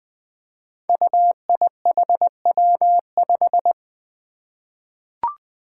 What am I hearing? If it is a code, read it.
Morse code, heard as a keyed tone, about 21 wpm, 700 Hz. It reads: UIHW5
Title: Five Challenging Letter-Number Combinations 20wpm